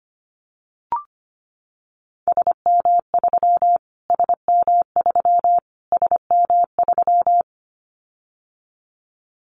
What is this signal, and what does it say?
Morse code, heard as a keyed tone, about 25 wpm, 700 Hz. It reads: SM3 SM3 SM3
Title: Callsign Prefixes for Top 25 DXCC - Repeated 3x - 25wpm